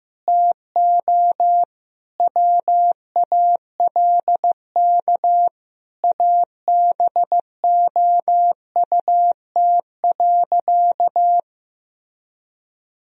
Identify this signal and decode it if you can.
Morse code, heard as a keyed tone, about 15 wpm, 700 Hz. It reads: TO WALK ABOUT.
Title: Sentences from Top 1000 Words - Review Part 1 - 15wpm